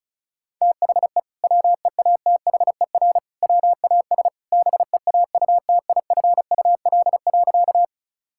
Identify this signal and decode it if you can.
Morse code, heard as a keyed tone, about 36 wpm, 700 Hz. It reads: THE WEATHER WAS BEAUTIFUL.
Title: Sentences from Top 1000 Words - Review Part 5 - 35wpm